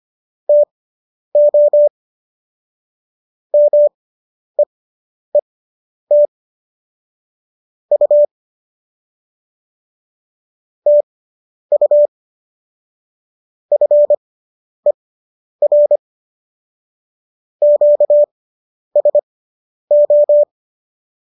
Morse code, heard as a keyed tone, about 25 wpm, spaced as for 10 wpm, 590 Hz. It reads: TO MEET U TU FER QSO